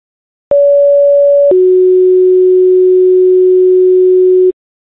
Two-tone page, heard as a one-second tone, then a three-second tone, about 570 then 370 Hz. It is heard from 0.5 to 4.5 s.